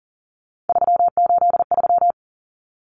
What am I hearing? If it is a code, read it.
Morse code, heard as a keyed tone, about 38 wpm, 700 Hz. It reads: 383
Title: Numbers - 3-Digit 40wpm